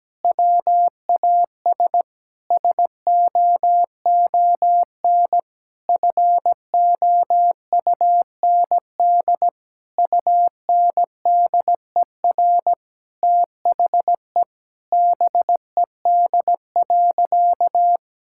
Morse code, heard as a keyed tone, about 17 wpm, 700 Hz. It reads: WAS SOON FOUND UNDER THE BED.
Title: Sentences from Top 1000 Words - Review Part 5 - 17wpm